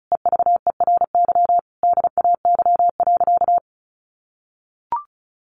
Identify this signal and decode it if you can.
Morse code, heard as a keyed tone, about 34 wpm, 700 Hz. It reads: EVERY DAY.
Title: Sentences from Top 700 Words 35wpm